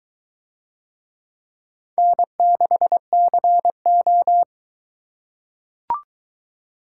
Morse code, heard as a keyed tone, about 22 wpm, 700 Hz. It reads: N6CO